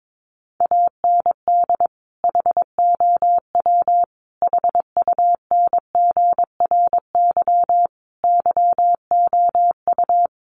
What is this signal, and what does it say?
Morse code, heard as a keyed tone, about 22 wpm, 700 Hz. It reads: AND HOW HUNGRY YOU
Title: Sentences from Top 1400 Words 22wpm